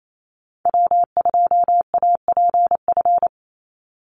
Morse code, heard as a keyed tone, about 28 wpm, 700 Hz. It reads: W2APF
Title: Call Signs Repeated 3x - 28wpm